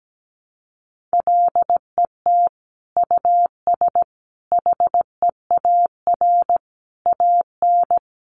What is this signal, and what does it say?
Morse code, heard as a keyed tone, about 17 wpm, 700 Hz. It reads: LET US HEAR AN
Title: Sentences from Top 900 Words 17wpm